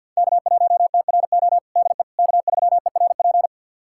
Morse code, heard as a keyed tone, about 52 wpm, 700 Hz. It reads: K1TRO DE K3ERP
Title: USA Callsign DE Callsign 50wpm